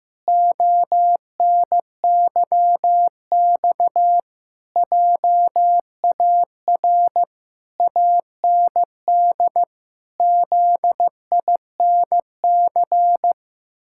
Morse code, heard as a keyed tone, about 15 wpm, 700 Hz. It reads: ONYX JAR AND ZINC
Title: Pangram Challenge v2 - 15wpm